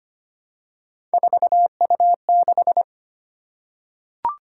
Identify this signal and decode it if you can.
Morse code, heard as a keyed tone, about 25 wpm, 700 Hz. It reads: 4U6
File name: Three Letter-Number Combinations - 25wpm